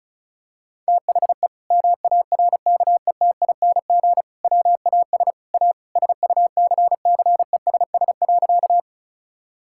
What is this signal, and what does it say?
Morse code, heard as a keyed tone, about 36 wpm, 700 Hz. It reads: THE MARKETING WAS A SUCCESS.
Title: Sentences from Top 1000 Words - Review Part 5 - 35wpm